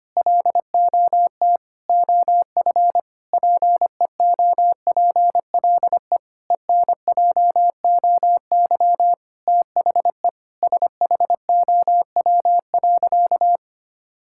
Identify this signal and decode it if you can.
Morse code, heard as a keyed tone, about 25 wpm, 700 Hz. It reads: LOT OF PEOPLE ENJOY THE SHOW.
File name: practice / 25wpm / Sentences with 4-Word Phrases - 25wpm